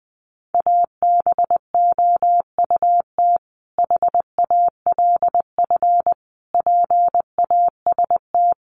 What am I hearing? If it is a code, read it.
Morse code, heard as a keyed tone, about 19 wpm, 700 Hz. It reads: ABOUT HALF PAST
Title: Sentences from Top 600 Words 20wpm